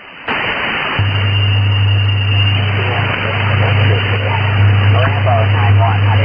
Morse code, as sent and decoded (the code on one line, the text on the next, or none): none